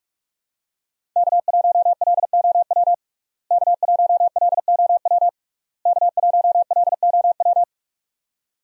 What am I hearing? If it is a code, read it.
Morse code, heard as a keyed tone, about 41 wpm, 700 Hz. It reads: K1LOW K1LOW K1LOW
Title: USA Call Signs - 5 Characters - Repeated 3x - 45wpm